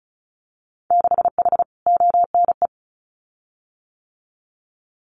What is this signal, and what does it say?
Morse code, heard as a keyed tone, about 36 wpm, 700 Hz. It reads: 6H ONE
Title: Winter Field Day Exchanges Encore 35wpm